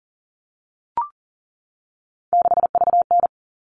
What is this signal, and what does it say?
Morse code, heard as a keyed tone, about 40 wpm, 700 Hz. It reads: 6VN
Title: All Cut Numbers - 3-Digit 40wpm